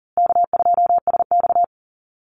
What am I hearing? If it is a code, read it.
Morse code, heard as a keyed tone, about 40 wpm, 700 Hz. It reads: K2SX
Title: Call Signs Repeated 2x - 40wpm